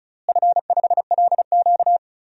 Morse code, heard as a keyed tone, about 36 wpm, 700 Hz. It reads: F5LQ
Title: DXCC-100 Call Signs - 4 Characters - Repeated 3x - 35wpm